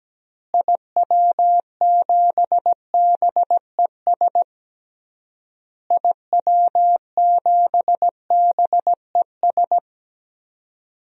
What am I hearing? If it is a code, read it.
Morse code, heard as a keyed tone, about 17 wpm, 700 Hz. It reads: IW7BES IW7BES
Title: DXCC-100 Call Signs - 6 Characters - Repeated 3x - 17wpm